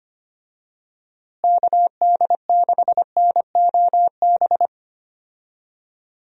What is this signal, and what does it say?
Morse code, heard as a keyed tone, about 25 wpm, 700 Hz. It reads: KD6NOB
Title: USA Call Signs - 6 Characters - Repeated 3x - 25wpm